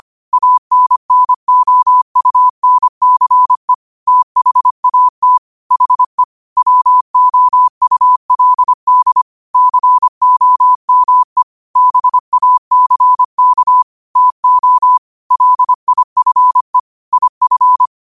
Morse code, heard as a keyed tone, about 25 wpm, 1000 Hz. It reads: ANNOUNCE THAT HE WOULD COME BACK TO LIFE IF
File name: qotdfiles / 25wpm